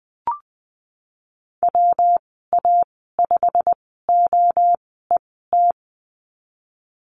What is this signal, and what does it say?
Morse code, heard as a keyed tone, about 20 wpm, spaced as for 15 wpm, 700 Hz. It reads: WA5OET
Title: 167 Lesson 27g - Number 7 - Call Signs 20wpm